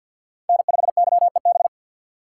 Morse code, heard as a keyed tone, about 53 wpm, 700 Hz. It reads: N5YEB
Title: USA Call Signs - 5 Characters - 3x Character Spacing - 50wpm